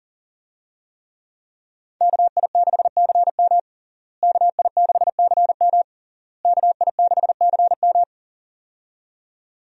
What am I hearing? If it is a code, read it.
Morse code, heard as a keyed tone, about 40 wpm, 700 Hz. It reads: KI6CM KI6CM KI6CM
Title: USA Call Signs - 5 Characters - Repeated 3x - 40wpm